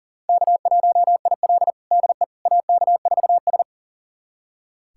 Morse code, heard as a keyed tone, about 37 wpm, 700 Hz. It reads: K1IL DE AK4S
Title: USA Callsign DE Callsign - 4 Characters - 40wpm